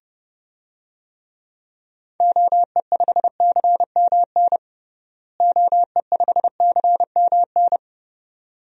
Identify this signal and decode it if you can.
Morse code, heard as a keyed tone, about 32 wpm, 700 Hz. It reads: OE5CMN OE5CMN
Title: DXCC-100 Call Signs - 6 Characters - Repeated 2x - 30wpm